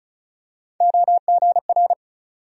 Morse code, heard as a keyed tone, about 36 wpm, 700 Hz. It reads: OGR
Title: USA Call Sign Postfixes - 3 Letters 35wpm